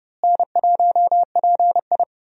Morse code, heard as a keyed tone, about 29 wpm, 700 Hz. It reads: N1PI
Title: USA Call Signs - 4 Characters - Repeated 2x - 30wpm